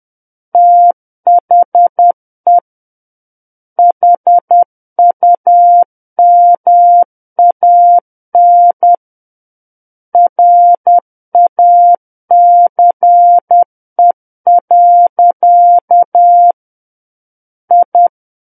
Morse code, heard as a keyed tone, about 10 wpm, 700 Hz. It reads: THE HUMAN RACE. I
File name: Book One / 01-Chapter 1 - 10 WPM